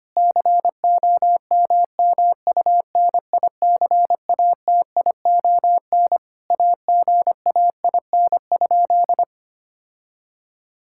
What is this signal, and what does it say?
Morse code, heard as a keyed tone, about 25 wpm, 700 Hz. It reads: COMMUNICATION AGAIN?
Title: Sentences from Top 1000 Words - Review Part 6 - 25wpm